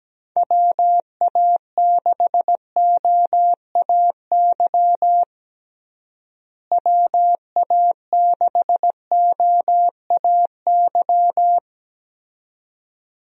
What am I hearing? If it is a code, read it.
Morse code, heard as a keyed tone, about 17 wpm, 700 Hz. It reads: WA6OAY WA6OAY
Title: USA Call Signs - 6 Characters - Repeated 2x - 17wpm